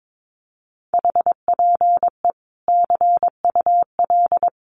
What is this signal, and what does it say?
Morse code, heard as a keyed tone, about 22 wpm, 700 Hz. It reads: HPE CUL
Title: Common QSO Phrases 22wpm